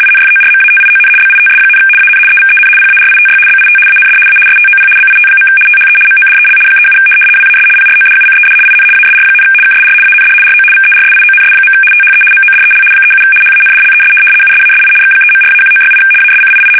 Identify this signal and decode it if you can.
RTTY, about 75 baud. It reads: INFORMATION BITS, AN EVEN PARITY BIT AND A STOP BIT. THE MODEL 37 WAS PRODUCED IN ASR, KSR AND RO VERSIONS. THE MODEL 37 HANDLES USASI X3.4-1968 ASCII
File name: fsk850-75B / teletype-corp